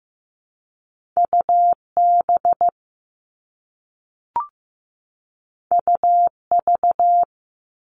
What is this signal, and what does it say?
Morse code, heard as a keyed tone, about 15 wpm, 700 Hz. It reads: UB UV
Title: All Cut Numbers - 2-Digit 15wpm